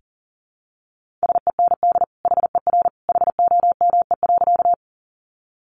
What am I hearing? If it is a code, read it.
Morse code, heard as a keyed tone, about 40 wpm, 700 Hz. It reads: SEND HER HOME.
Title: Sentences from Top 1500 Words 40wpm